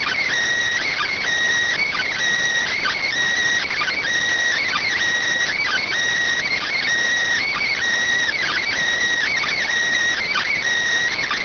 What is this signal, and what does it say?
Emergency Alert System (SAME) tones (header bursts), heard as harsh buzzing bursts. It is heard from 0.8 to 10.6 s.